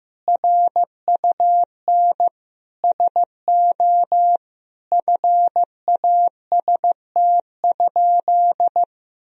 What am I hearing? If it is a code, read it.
Morse code, heard as a keyed tone, about 15 wpm, 700 Hz. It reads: RUN SO FAST?
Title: Sentences from Top 1000 Words - Review Part 2 - 15wpm